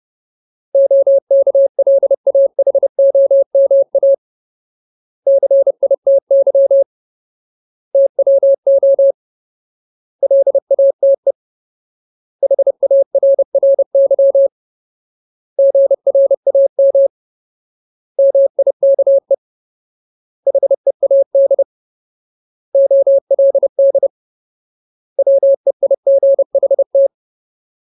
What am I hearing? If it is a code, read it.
Morse code, heard as a keyed tone, about 30 wpm, 550 Hz. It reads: OKLAHOMA CITY TWO LATE HARRY GRAM MIKE HEAD OLD WEIGHT
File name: Ultimate Word List / 30WPM Ultimate